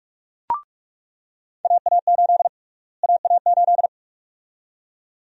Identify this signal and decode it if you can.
Morse code, heard as a keyed tone, about 46 wpm, 700 Hz. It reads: AA8 AA8
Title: USA Call Sign Prefixes - Repeated 2x - 45wpm